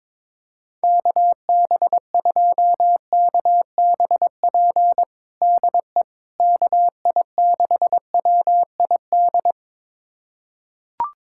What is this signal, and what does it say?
Morse code, heard as a keyed tone, about 22 wpm, 700 Hz. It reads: KB2KBP DE KI6WID
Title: USA Callsign DE Callsign - 6 Characters - 22wpm